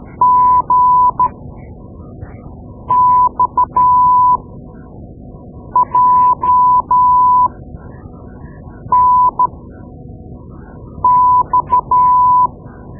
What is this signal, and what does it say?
Morse code, heard as a keyed tone, about 12 wpm, spaced as for 5 wpm, 1000 Hz. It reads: GXJNX
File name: Morse Lessons / Morse Lessons / Tape-2B